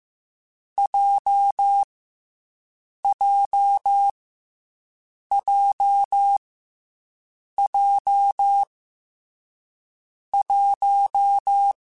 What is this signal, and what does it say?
Morse code, heard as a keyed tone, about 15 wpm, 800 Hz. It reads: J J J J 1
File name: Difficult Combos / Difficult Combos-mp3s / WPJ WPJ1 @13